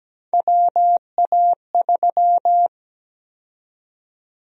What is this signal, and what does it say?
Morse code, heard as a keyed tone, about 17 wpm, 700 Hz. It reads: WA3
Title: USA Call Sign Prefixes - 17wpm